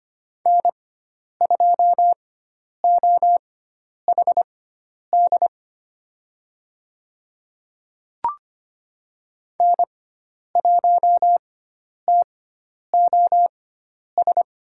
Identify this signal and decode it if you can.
Morse code, heard as a keyed tone, about 25 wpm, spaced as for 10 wpm, 700 Hz. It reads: N2OHD N1TOS
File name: Lesson 16g - Letter c - Call Signs 25wpm